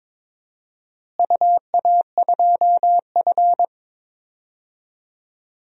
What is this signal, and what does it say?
Morse code, heard as a keyed tone, about 21 wpm, 700 Hz. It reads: UA2F
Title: DXCC-100 Call Signs - 4 Characters - Repeated 3x - 22wpm